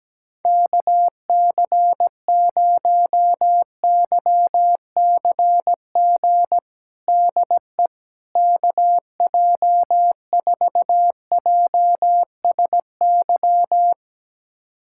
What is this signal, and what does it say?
Morse code, heard as a keyed tone, about 17 wpm, 700 Hz. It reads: KC0YCG DE KJ4JSY
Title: USA Callsign DE Callsign - 6 Characters - 17wpm